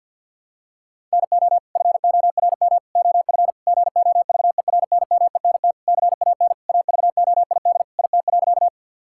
Morse code, heard as a keyed tone, about 50 wpm, 700 Hz. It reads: NO FORM OF GOVERNMENT CAN AVOID IT.
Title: Sentences from Top 1000 Words - Review Part 6 - 50wpm